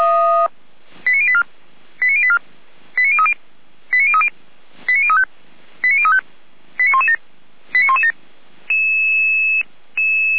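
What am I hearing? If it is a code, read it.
Five-tone selective call (ZVEI), heard as quick runs of five tones, tones of about 70 ms. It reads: 89084 89084 89030 89030 89035 89035 89117 89117
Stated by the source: ZVEI five-tone selective call received by SDR.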